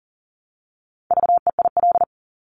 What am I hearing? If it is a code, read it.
Morse code, heard as a keyed tone, about 38 wpm, 700 Hz. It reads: VEIL
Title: Multi-letter Phonemes Repeated 3x 40wpm